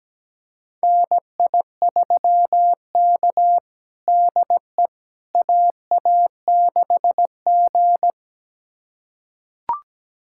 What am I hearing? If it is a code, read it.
Morse code, heard as a keyed tone, about 17 wpm, 700 Hz. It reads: NI3K DE AA6G
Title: USA Callsign DE Callsign - 4 Characters - 17wpm